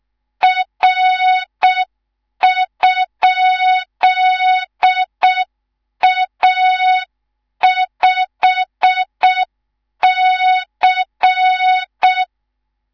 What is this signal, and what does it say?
Morse code, heard as a keyed tone, about 6 wpm, 3700 Hz. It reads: R?A5C
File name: Morze-HA2MN / hang / 19 GYAKORLAT